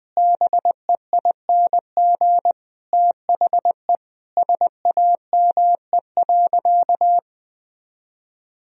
Morse code, heard as a keyed tone, about 20 wpm, 700 Hz. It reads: BEING THE SAME.
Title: Sentences from Top 1000 Words - Review Part 3 - 20wpm